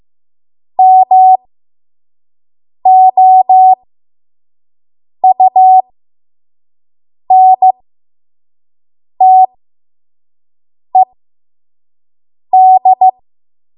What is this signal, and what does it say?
Morse code, heard as a keyed tone, about 15 wpm, spaced as for 5 wpm, 750 Hz. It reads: MOUNTED
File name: Morse / 05WPM